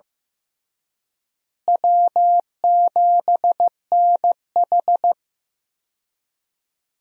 Morse code, heard as a keyed tone, about 15 wpm, 700 Hz. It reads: W7NH
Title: USA Call Signs - 4 Characters - Repeated 3x - 15wpm